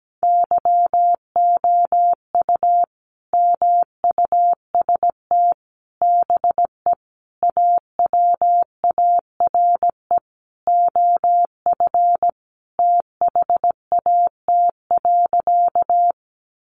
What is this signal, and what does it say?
Morse code, heard as a keyed tone, about 17 wpm, 700 Hz. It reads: YOU MUST BE AWARE OF THAT.
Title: Sentences from Top 1200 Words 17wpm